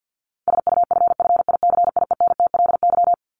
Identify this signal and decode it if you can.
Morse code, heard as a keyed tone, about 49 wpm, 700 Hz. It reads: SUFFICIENTLY
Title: LY Suffix - Top 100 Words with Base Words 50wpm